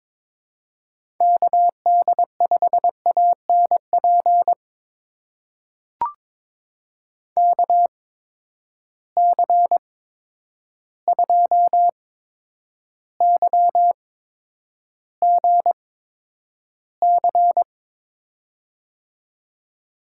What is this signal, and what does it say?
Morse code, heard as a keyed tone, about 22 wpm, spaced as for 9 wpm, 700 Hz. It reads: KD5ANP KC2YGC
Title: USA Call Signs - 6 Characters - 8x Character Spacing - 22wpm